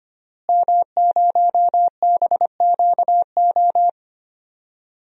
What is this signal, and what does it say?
Morse code, heard as a keyed tone, about 25 wpm, 700 Hz. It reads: M0BQO
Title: DXCC-100 Call Signs - 5 Characters - Repeated 2x - 25wpm